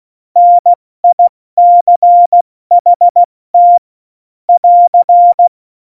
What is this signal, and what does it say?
Morse code, heard as a keyed tone, about 16 wpm, 700 Hz. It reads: NICHT <AR>
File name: Morsekurs / Witze / Witz 13. August 2019, 70 F80